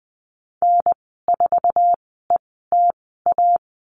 Morse code, heard as a keyed tone, about 20 wpm, spaced as for 15 wpm, 700 Hz. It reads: N4ETA
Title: 055 Lesson 10g - Letter r - Call Signs 20wpm